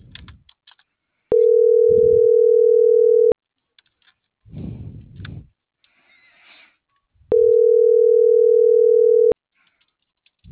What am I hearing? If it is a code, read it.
Ringback tone (North America), 440 + 480 Hz.